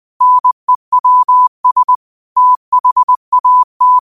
Morse code, heard as a keyed tone, about 20 wpm, 1000 Hz. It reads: NEWS THAT
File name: qotdfiles / 20wpm